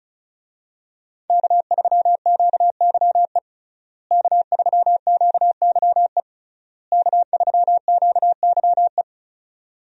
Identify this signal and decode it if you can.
Morse code, heard as a keyed tone, about 36 wpm, 700 Hz. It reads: K3QYE K3QYE K3QYE
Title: USA Call Signs - 5 Characters - Repeated 3x - 35wpm